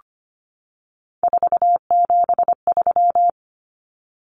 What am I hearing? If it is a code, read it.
Morse code, heard as a keyed tone, about 25 wpm, 700 Hz. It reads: 473
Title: Numbers - 3-Digit 25wpm